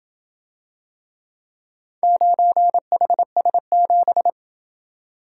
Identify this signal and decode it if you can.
Morse code, heard as a keyed tone, about 28 wpm, 700 Hz. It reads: 9HS7